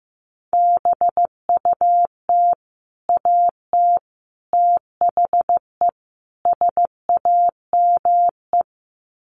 Morse code, heard as a keyed tone, about 15 wpm, 700 Hz. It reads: BUT AT THE SAME
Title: Sets of 4 Words 15wpm